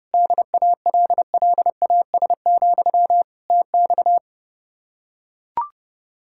Morse code, heard as a keyed tone, about 31 wpm, 700 Hz. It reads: DALLAS, TX